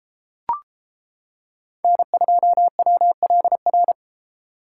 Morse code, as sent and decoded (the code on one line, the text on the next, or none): -. ..--- .-- .-.. .-.
N2WLR